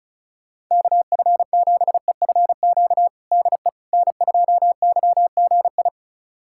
Morse code, heard as a keyed tone, about 34 wpm, 700 Hz. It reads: KF7EFQ DE N2YGI
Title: USA Callsign DE Callsign 35wpm